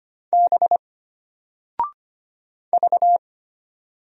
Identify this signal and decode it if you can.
Morse code, heard as a keyed tone, about 25 wpm, 700 Hz. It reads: B V
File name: Single-Letter-Number-25wpm